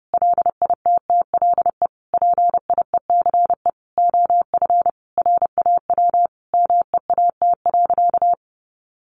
Morse code, heard as a keyed tone, about 30 wpm, 700 Hz. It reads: LITTLE PIECE OF RAW MEAT.
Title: Sentences from Top 1500 Words 30wpm